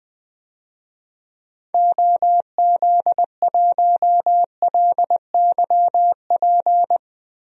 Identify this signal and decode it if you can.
Morse code, heard as a keyed tone, about 20 wpm, 700 Hz. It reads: OZ1LYP
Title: DXCC-100 Call Signs - 6 Characters - Repeated 3x - 20wpm